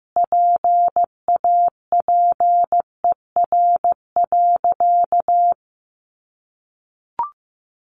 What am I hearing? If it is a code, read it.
Morse code, heard as a keyed tone, about 15 wpm, 700 Hz. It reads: PAPER.
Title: Sentences from Top 1000 Words 15wpm